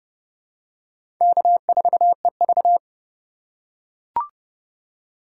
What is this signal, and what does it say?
Morse code, heard as a keyed tone, about 30 wpm, 700 Hz. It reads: K4EV
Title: USA Call Signs - 4 Characters - 3x Character Spacing - 30wpm